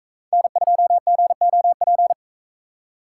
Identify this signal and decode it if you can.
Morse code, heard as a keyed tone, about 42 wpm, 700 Hz. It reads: N2GOP